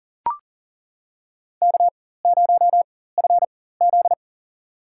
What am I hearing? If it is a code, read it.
Morse code, heard as a keyed tone, about 40 wpm, spaced as for 19 wpm, 700 Hz. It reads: K0FZ
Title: USA Call Signs - 4 Characters - 4x Character Spacing - 40wpm